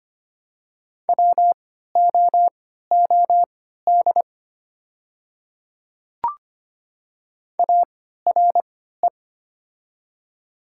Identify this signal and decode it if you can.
Morse code, heard as a keyed tone, about 25 wpm, spaced as for 14 wpm, 700 Hz. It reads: WOOD ARE
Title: 3x Character Spacing - Top 300 Words - 25wpm